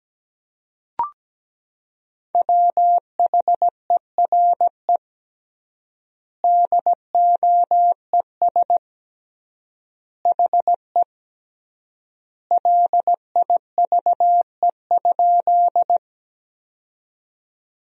Morse code, heard as a keyed tone, about 17 wpm, 700 Hz. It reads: WHERE DOES HE LIVE?
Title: 3x Word-Spacing - Sentences from Top 200 Words - 17wpm